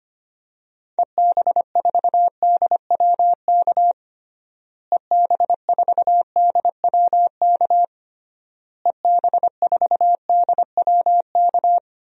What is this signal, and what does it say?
Morse code, heard as a keyed tone, about 25 wpm, 700 Hz. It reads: EB4DWK EB4DWK EB4DWK
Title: DXCC-100 Call Signs - 6 Characters - Repeated 3x - 25wpm